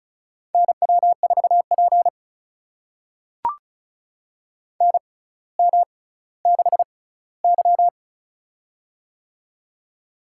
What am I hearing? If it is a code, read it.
Morse code, heard as a keyed tone, about 36 wpm, spaced as for 18 wpm, 700 Hz. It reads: NW4P NM6Y
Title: USA Call Signs - 4 Characters - 6x Character Spacing - 35wpm